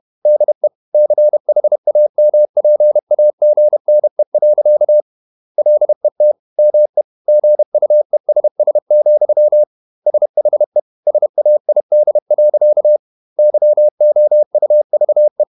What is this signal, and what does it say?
Morse code, heard as a keyed tone, about 31 wpm, 600 Hz. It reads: DE CHAMPAGNE. LET ME GUESS, SHE SAID. YOUVE
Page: Never 31wpm | CW med Gnister